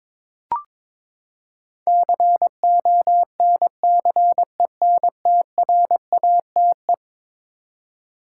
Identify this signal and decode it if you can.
Morse code, heard as a keyed tone, about 22 wpm, 700 Hz. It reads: CONCENTRATE
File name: practice / 22wpm / Eleven Letter Words - 22wpm